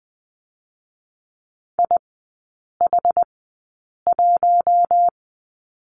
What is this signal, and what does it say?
Morse code, heard as a keyed tone, about 20 wpm, spaced as for 8 wpm, 690 Hz. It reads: IH1
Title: 060 Lesson 11e - Letter h - Three Characters 20wpm